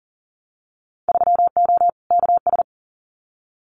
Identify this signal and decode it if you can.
Morse code, heard as a keyed tone, about 41 wpm, 700 Hz. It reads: 3O KS
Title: Winter Field Day Exchanges Encore 40wpm